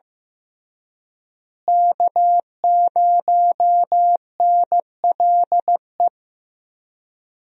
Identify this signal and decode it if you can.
Morse code, heard as a keyed tone, about 15 wpm, 700 Hz. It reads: K0NLE
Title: USA Call Signs - 5 Characters - Repeated 3x - 15wpm